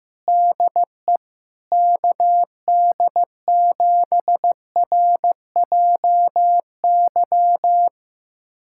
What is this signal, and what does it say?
Morse code, heard as a keyed tone, about 16 wpm, 700 Hz. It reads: DE KD7RJY
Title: USA Callsign DE Callsign - 6 Characters - 15wpm